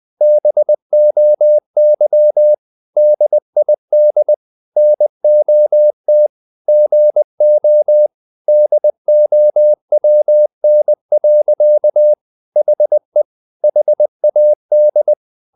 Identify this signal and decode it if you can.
Morse code, heard as a keyed tone, about 20 wpm, 600 Hz. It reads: BOY DID NOT GO DOWN. HE HAD
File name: the old man and the sea part 2 20wpm